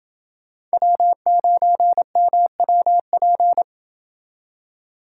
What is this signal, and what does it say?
Morse code, heard as a keyed tone, about 28 wpm, 700 Hz. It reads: W9MWP